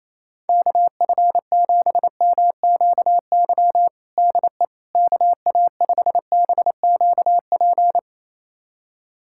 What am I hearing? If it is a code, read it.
Morse code, heard as a keyed tone, about 28 wpm, 700 Hz. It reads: KF7MQY DE KA5BQP
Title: USA Callsign DE Callsign - 6 Characters - 28wpm